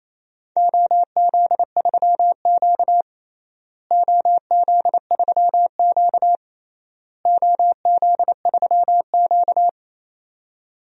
Morse code, heard as a keyed tone, about 27 wpm, 700 Hz. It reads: OZ3Q OZ3Q OZ3Q
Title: DXCC-100 Call Signs - 4 Characters - Repeated 3x - 28wpm